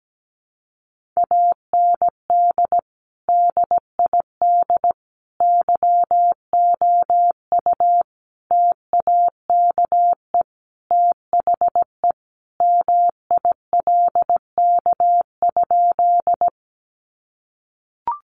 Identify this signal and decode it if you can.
Morse code, heard as a keyed tone, about 17 wpm, 700 Hz. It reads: AND DID YOU TAKE THE MILK?
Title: Sentences from Top 1500 Words 17wpm